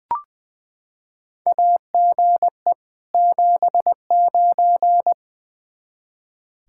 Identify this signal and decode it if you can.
Morse code, heard as a keyed tone, about 20 wpm, 700 Hz. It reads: AGE 79
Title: K1USN SST QSO Practice 20wpm